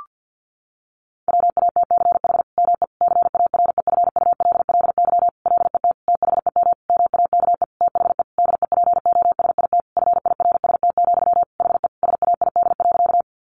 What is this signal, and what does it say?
Morse code, heard as a keyed tone, about 50 wpm, 700 Hz. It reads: WATCH ME CAREFULLY LET THEM MAKE THE DEPOSIT FIRST, HE SAID.
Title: Sentences from Top 1300 Words 50wpm